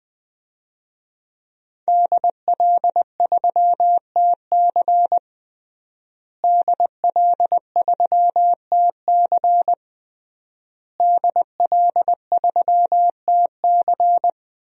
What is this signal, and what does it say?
Morse code, heard as a keyed tone, about 20 wpm, 700 Hz. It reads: DL3TC DL3TC DL3TC
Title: DXCC-100 Call Signs - 5 Characters - Repeated 3x - 20wpm